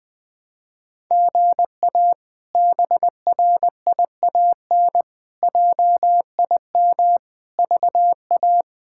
Morse code, heard as a keyed tone, about 20 wpm, 700 Hz. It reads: GA BRIAN JIM VA
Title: K1USN SST QSO Practice - 20wpm